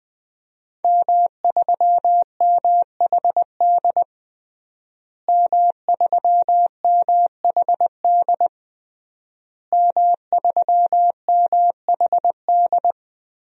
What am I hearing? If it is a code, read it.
Morse code, heard as a keyed tone, about 20 wpm, 700 Hz. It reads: M3MHD M3MHD M3MHD
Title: DXCC-100 Call Signs - 5 Characters - Repeated 3x - 20wpm